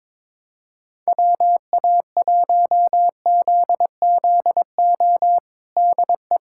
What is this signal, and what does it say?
Morse code, heard as a keyed tone, about 23 wpm, 700 Hz. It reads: WA1ZZO DE
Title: USA Callsign DE Callsign - 6 Characters - 22wpm